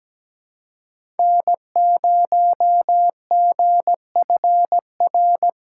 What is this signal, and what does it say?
Morse code, heard as a keyed tone, about 17 wpm, 700 Hz. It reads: N0GFR
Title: USA Call Signs - 5 Characters - 6x Character Spacing - 17wpm